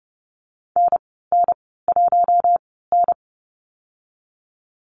Morse code, heard as a keyed tone, about 30 wpm, spaced as for 17 wpm, 700 Hz. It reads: NN1N
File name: Lesson 14g - Number 2 - Call Signs 30wpm